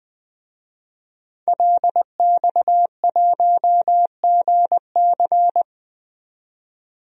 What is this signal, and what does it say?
Morse code, heard as a keyed tone, about 20 wpm, 700 Hz. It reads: LX1GC
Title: DXCC-100 Call Signs - 5 Characters - 20wpm